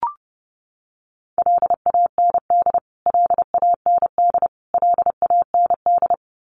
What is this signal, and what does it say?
Morse code, heard as a keyed tone, about 30 wpm, 700 Hz. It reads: LAND LAND LAND
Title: Top 200 Words Repeated 30wpm